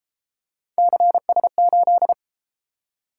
Morse code, heard as a keyed tone, about 32 wpm, 700 Hz. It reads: CS8